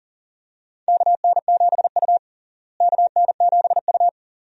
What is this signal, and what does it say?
Morse code, heard as a keyed tone, about 40 wpm, 700 Hz. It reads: KN7U KN7U
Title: USA Call Signs - 4 Characters - Repeated 3x - 40wpm